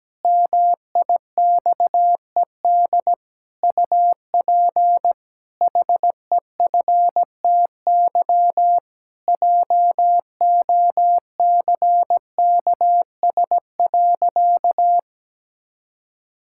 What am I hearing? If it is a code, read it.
Morse code, heard as a keyed tone, about 17 wpm, 700 Hz. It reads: MIXED UP HEFTY JOCKS.
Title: Pangram Challenge v2 - 17wpm